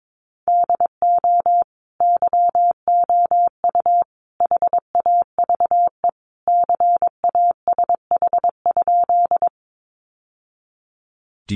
Morse code, heard as a keyed tone, about 22 wpm, 700 Hz. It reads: DO YOU HAVE CASH?
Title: Sentences from Top 700 Words 22wpm